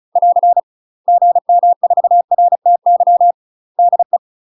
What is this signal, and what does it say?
Morse code, heard as a keyed tone, about 36 wpm, 700 Hz. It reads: <AR> GM4RTY DE
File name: UK MASTER QSOs TRAINING / 35WPM / UK MASTER QSOs 0026